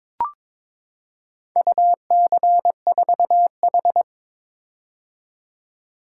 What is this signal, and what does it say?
Morse code, heard as a keyed tone, about 22 wpm, 690 Hz. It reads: UC4H
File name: practice / 22wpm / Four Letter-Number Combinations - 22wpm